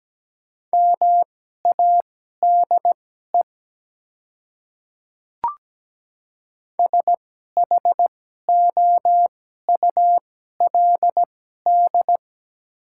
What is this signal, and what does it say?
Morse code, heard as a keyed tone, about 17 wpm, spaced as for 12 wpm, 700 Hz. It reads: MADE SHOULD
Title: 2x Character Spacing - Top 300 Words - 17wpm